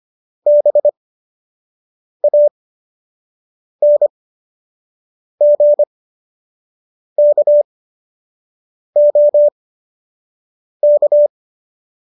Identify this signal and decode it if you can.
Morse code, heard as a keyed tone, about 25 wpm, spaced as for 6 wpm, 590 Hz. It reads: BANGKOK